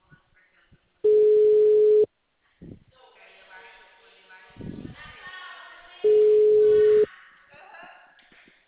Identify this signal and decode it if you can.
Ringback tone (Europe, 425 Hz), with speech.